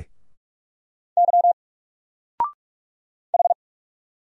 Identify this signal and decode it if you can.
Morse code, heard as a keyed tone, about 46 wpm, 700 Hz. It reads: Y H
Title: Single Letter-Number Alt 45wpm